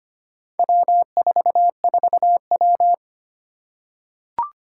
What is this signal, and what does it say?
Morse code, heard as a keyed tone, about 25 wpm, 700 Hz. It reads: W44W
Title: Four Challenging Letter-Number Combinations 25wpm